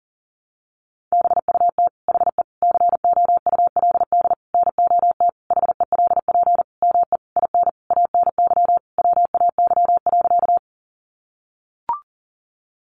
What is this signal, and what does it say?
Morse code, heard as a keyed tone, about 40 wpm, 700 Hz. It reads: BUT HE COULD NOT HELP ME IN ANY WAY.
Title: Sentences from Top 600 Words 40wpm